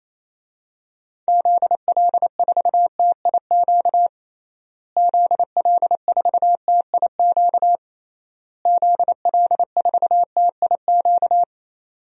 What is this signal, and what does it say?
Morse code, heard as a keyed tone, about 27 wpm, 700 Hz. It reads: ZL4TIQ ZL4TIQ ZL4TIQ
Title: DXCC-100 Call Signs - 6 Characters - Repeated 3x - 28wpm